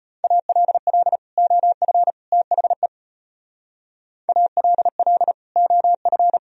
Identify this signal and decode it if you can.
Morse code, heard as a keyed tone, about 31 wpm, 700 Hz. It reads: ALL OF THE ALL OF
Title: Speed-Racing - Sets of 3 Words 25wpm